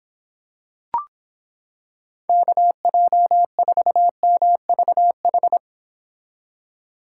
Morse code, heard as a keyed tone, about 26 wpm, 700 Hz. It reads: KJ4MVH